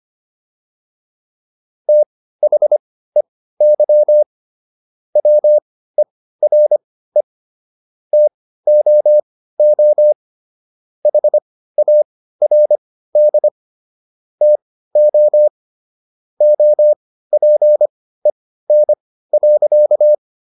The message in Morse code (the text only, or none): THEY WERE TOO HARD TO OPEN.